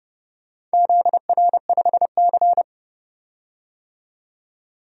Morse code, heard as a keyed tone, about 32 wpm, 700 Hz. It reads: ZR5C
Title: DXCC-100 Call Signs - 4 Characters - 30wpm